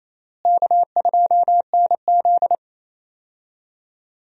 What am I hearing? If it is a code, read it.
Morse code, heard as a keyed tone, about 28 wpm, 700 Hz. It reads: K2NZ
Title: USA Call Signs - 4 Characters - Repeated 3x - 28wpm